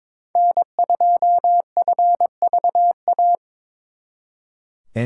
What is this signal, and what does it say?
Morse code, heard as a keyed tone, about 26 wpm, 700 Hz. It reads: N2FVA
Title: Speed Racing - USA 1x3 Call Signs 22wpm